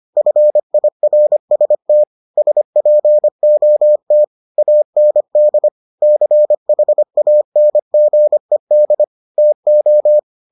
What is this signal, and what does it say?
Morse code, heard as a keyed tone, about 25 wpm, 600 Hz. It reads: FIRST SPOT AND CHANGED TO